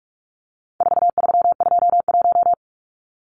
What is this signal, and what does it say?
Morse code, heard as a keyed tone, about 44 wpm, 700 Hz. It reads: 4321
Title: Numbers - 4-Digit 45wpm